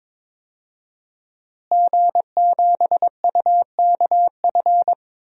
Morse code, heard as a keyed tone, about 21 wpm, 700 Hz. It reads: G7UKF
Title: DXCC-100 Call Signs - 5 Characters - Repeated 2x - 22wpm